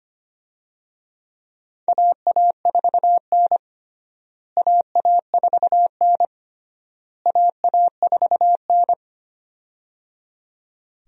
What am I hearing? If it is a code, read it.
Morse code, heard as a keyed tone, about 25 wpm, 700 Hz. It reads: AA4N AA4N AA4N
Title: USA Call Signs - 4 Characters - Repeated 3x - 25wpm